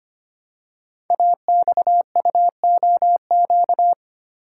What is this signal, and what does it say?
Morse code, heard as a keyed tone, about 25 wpm, 700 Hz. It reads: AXUOQ
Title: Code Groups - 5 Letters - 25wpm